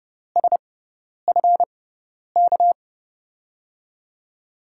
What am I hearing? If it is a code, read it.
Morse code, heard as a keyed tone, about 30 wpm, spaced as for 10 wpm, 700 Hz. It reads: SF K
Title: USA Call Signs - 6x Character Spacing - 30wpm